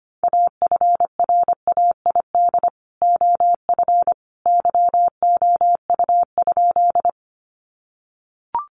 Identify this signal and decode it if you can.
Morse code, heard as a keyed tone, about 25 wpm, 700 Hz. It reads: AFRAID OF YOU?
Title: Sentences from Top 1500 Words 25wpm